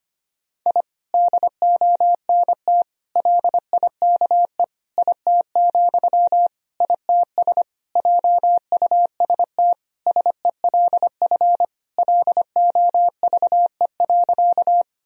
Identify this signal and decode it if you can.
Morse code, heard as a keyed tone, about 25 wpm, 700 Hz. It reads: I DONT LIKE IT, ITS JUST SELF LOVE.
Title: Sentences from Top 1000 Words - Review Part 1 - 25wpm